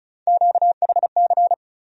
Morse code, heard as a keyed tone, about 36 wpm, 700 Hz. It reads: QHC
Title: USA Call Sign Postfixes - 3 Letters 35wpm